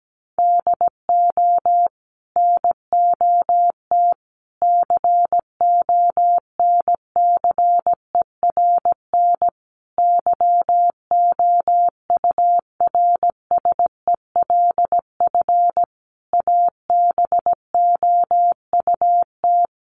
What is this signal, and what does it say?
Morse code, heard as a keyed tone, about 17 wpm, 700 Hz. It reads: DO NOT CONCERN YOURSELF ABOUT
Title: Sentences from Top 1000 Words 17wpm